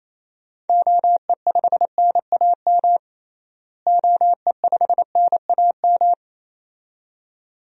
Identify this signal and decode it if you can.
Morse code, heard as a keyed tone, about 28 wpm, 700 Hz. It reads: OE5NAM OE5NAM
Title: DXCC-100 Call Signs - 6 Characters - Repeated 2x - 28wpm